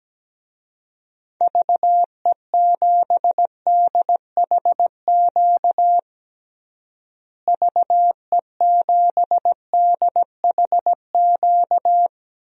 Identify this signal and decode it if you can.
Morse code, heard as a keyed tone, about 17 wpm, 700 Hz. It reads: VE7DHQ VE7DHQ
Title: DXCC-100 Call Signs - 6 Characters - Repeated 3x - 17wpm